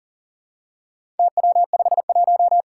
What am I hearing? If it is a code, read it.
Morse code, heard as a keyed tone, about 40 wpm, 700 Hz. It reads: TW51
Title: Four Challenging Letter-Number Combinations 40wpm